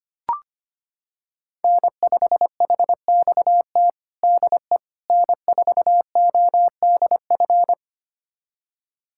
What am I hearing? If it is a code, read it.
Morse code, heard as a keyed tone, about 25 wpm, 700 Hz. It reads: N5HXT DE N4ODF
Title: USA Callsign DE Callsign - 5 Characters - 25wpm